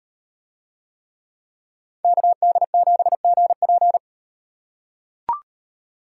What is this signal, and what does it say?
Morse code, heard as a keyed tone, about 35 wpm, 700 Hz. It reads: KD7GP